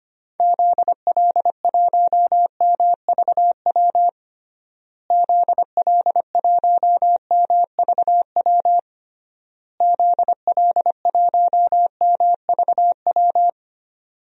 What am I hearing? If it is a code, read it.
Morse code, heard as a keyed tone, about 25 wpm, 700 Hz. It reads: ZL1MVW ZL1MVW ZL1MVW
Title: DXCC-100 Call Signs - 6 Characters - Repeated 3x - 25wpm